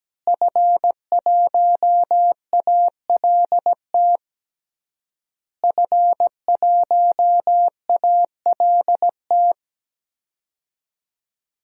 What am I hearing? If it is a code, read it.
Morse code, heard as a keyed tone, about 17 wpm, 700 Hz. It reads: F1ALT F1ALT
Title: DXCC-100 Call Signs - 5 Characters - Repeated 3x - 17wpm